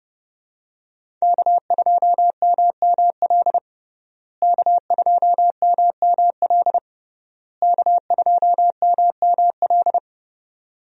Morse code, heard as a keyed tone, about 32 wpm, 700 Hz. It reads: K2MML K2MML K2MML
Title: USA Call Signs - 5 Characters - Repeated 3x - 30wpm